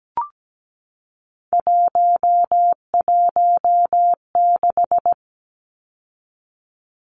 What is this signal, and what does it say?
Morse code, heard as a keyed tone, about 17 wpm, 700 Hz. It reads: 116
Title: Numbers - 3-Digit 17wpm